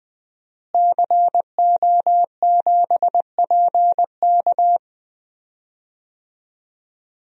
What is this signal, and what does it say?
Morse code, heard as a keyed tone, about 21 wpm, 700 Hz. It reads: CO7PK
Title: DXCC-100 Call Signs - 5 Characters - Repeated 2x - 20wpm